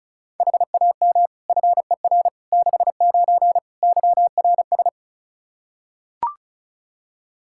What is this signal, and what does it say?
Morse code, heard as a keyed tone, about 34 wpm, 700 Hz. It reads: HAM FER 69 YRS
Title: K1USN SST QSO Practice 35wpm